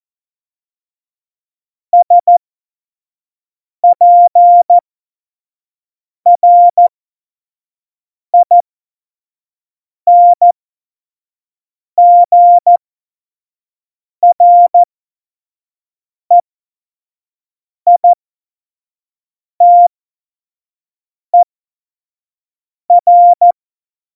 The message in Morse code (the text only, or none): SPRINGREITER